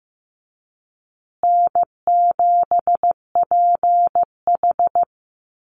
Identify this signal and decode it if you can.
Morse code, heard as a keyed tone, about 15 wpm, 700 Hz. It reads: N7PH
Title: USA Call Signs Repeated 3x 15wpm